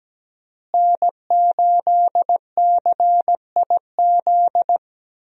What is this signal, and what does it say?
Morse code, heard as a keyed tone, about 17 wpm, 700 Hz. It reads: N8CIZ
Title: USA Call Signs - 5 Characters - Repeated 2x - 17wpm